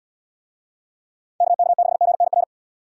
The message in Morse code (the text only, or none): KK6GNR